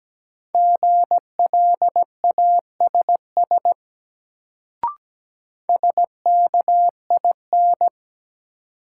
Morse code, heard as a keyed tone, about 17 wpm, 700 Hz. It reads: GLASS SKIN
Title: Sentences from Top 1300 Words - New Word Drill Alt - 17wpm